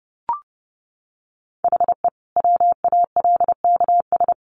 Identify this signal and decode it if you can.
Morse code, heard as a keyed tone, about 30 wpm, 700 Hz. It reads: HE WALKS
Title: Easy Phrases 30wpm